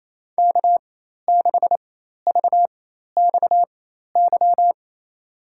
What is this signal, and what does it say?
Morse code, heard as a keyed tone, about 28 wpm, spaced as for 13 wpm, 700 Hz. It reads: K6VXY
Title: USA Call Signs - 5 Characters - 4x Character Spacing - 28wpm